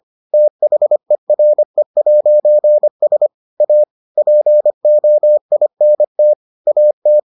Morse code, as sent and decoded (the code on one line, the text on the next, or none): - .... .-. .----. ... / .- / .--. --- .. -. - / .- -
THERE'S A POINT AT